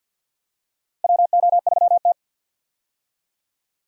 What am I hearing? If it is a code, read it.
Morse code, heard as a keyed tone, about 47 wpm, 700 Hz. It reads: WO3T
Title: USA Call Signs - 4 Characters - 50wpm